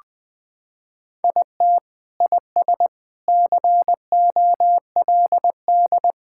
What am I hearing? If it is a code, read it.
Morse code, heard as a keyed tone, about 20 wpm, 700 Hz. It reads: IT IS COLD
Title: Sentences with 3-Word Phrases - 20wpm